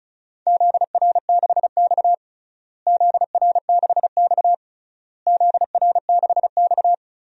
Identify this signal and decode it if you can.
Morse code, heard as a keyed tone, about 36 wpm, 700 Hz. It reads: ZR6X ZR6X ZR6X
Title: DXCC-100 Call Signs - 4 Characters - Repeated 3x - 35wpm